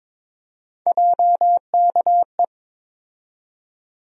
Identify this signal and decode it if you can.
Morse code, heard as a keyed tone, about 21 wpm, 700 Hz. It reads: JKE
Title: USA Call Sign Postfixes - 3 Letters 22wpm